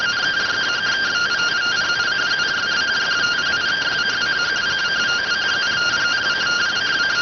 Two-tone teleprinter signal, about 170 Hz shift.